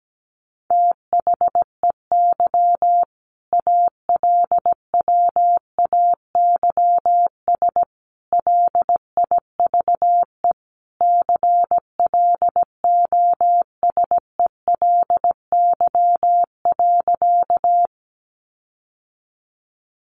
Morse code, heard as a keyed tone, about 17 wpm, 700 Hz. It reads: THEY ALWAYS LIVE CLOSELY.
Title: Sentences from Top 1400 Words 17wpm